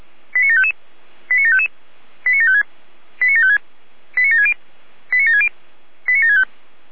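Five-tone selective call (ZVEI), heard as quick runs of five tones, tones of about 70 ms. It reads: 89755 89755 89756 89756 89760 89760 89765
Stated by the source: Five-tone ZVEI call 89752 received on an SDR.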